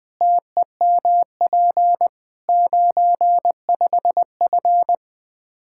Morse code, heard as a keyed tone, about 20 wpm, 700 Hz. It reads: TEMP 95F
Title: K1USN SST QSO Practice 20wpm